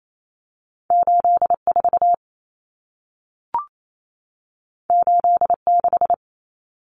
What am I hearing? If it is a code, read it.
Morse code, heard as a keyed tone, about 28 wpm, 700 Hz. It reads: 84 86
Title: Numbers - 2-Digit 28wpm